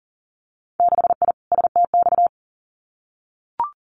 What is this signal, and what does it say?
Morse code, heard as a keyed tone, about 39 wpm, 700 Hz. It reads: 6I STX
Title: Winter Field Day Exchanges Encore 40wpm